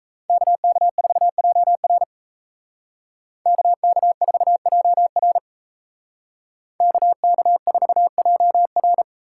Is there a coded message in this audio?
Yes: Morse code from 0.3 to 9.0 s, heard as a keyed tone, about 36 wpm, 700 Hz. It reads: KK4JR KK4JR KK4JR